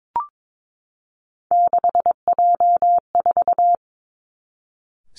Morse code, heard as a keyed tone, about 22 wpm, 700 Hz. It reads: 6J4
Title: Three Challenging Letter-Number Combinations 22wpm